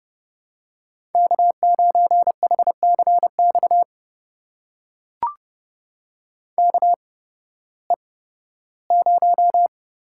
Morse code, heard as a keyed tone, about 30 wpm, spaced as for 16 wpm, 700 Hz. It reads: K9HCX KE0
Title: USA Call Signs - 8x Character Spacing - 30wpm